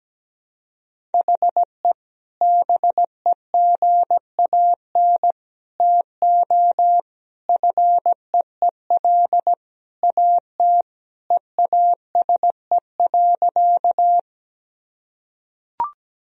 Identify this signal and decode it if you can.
Morse code, heard as a keyed tone, about 17 wpm, 700 Hz. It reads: HE BEGAN TO FEEL AT EASE.
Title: Sentences from Top 1000 Words - Review Part 1 - 17wpm